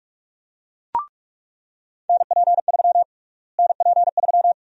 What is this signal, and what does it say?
Morse code, heard as a keyed tone, about 46 wpm, 700 Hz. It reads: NP3 NP3
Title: USA Call Sign Prefixes - Repeated 2x - 45wpm